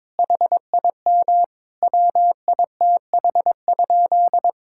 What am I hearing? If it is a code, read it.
Morse code, heard as a keyed tone, about 22 wpm, 700 Hz. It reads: HIM WITH?
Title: Sentences from Top 1000 Words - Review Part 5 - 22wpm